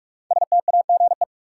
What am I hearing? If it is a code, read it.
Morse code, heard as a keyed tone, about 42 wpm, 700 Hz. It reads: STAGE
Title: Sentences from Top 1200 Words - New Word Drill - 45wpm